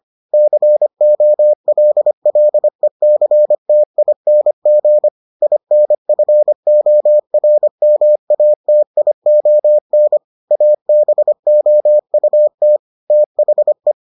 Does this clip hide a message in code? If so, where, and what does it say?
Yes: Morse code from 0.3 to 13.9 s, heard as a keyed tone, about 25 wpm, 600 Hz. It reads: COLLECTING INFORMATION ABOUT THE